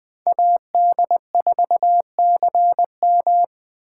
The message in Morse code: AD4CM

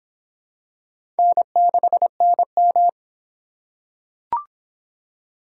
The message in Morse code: N6NM

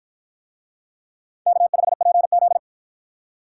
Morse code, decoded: K5PZ

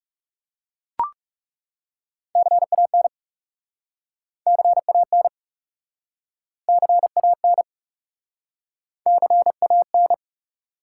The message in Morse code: CAN CAN CAN CAN

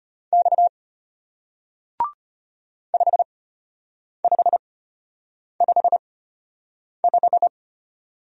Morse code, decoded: X 5555

